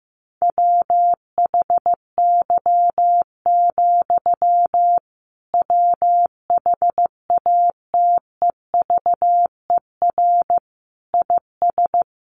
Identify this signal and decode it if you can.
Morse code, heard as a keyed tone, about 15 wpm, 700 Hz. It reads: WHY, WHATEVER IS
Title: Sentences from Top 1100 Words 15wpm